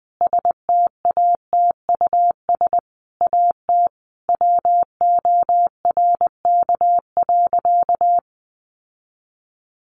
Morse code, heard as a keyed tone, about 20 wpm, 700 Hz. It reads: STATUS AT WORK.
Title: Sentences from Top 1400 Words 20wpm